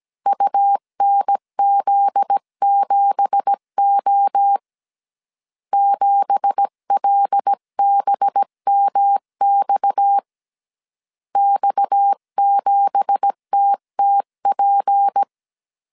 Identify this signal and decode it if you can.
Morse code, heard as a keyed tone, about 17 wpm, 800 Hz. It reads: UNZ7O 7LBMX X7TTP
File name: 1500C S B 16wpm